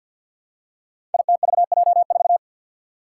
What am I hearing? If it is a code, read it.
Morse code, heard as a keyed tone, about 52 wpm, 700 Hz. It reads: ITVJ4
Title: Five Challenging Letter-Number Combinations 50wpm